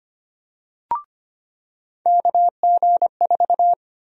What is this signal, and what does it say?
Morse code, heard as a keyed tone, about 25 wpm, 700 Hz. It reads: KG4
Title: USA Call Sign Prefixes - 25wpm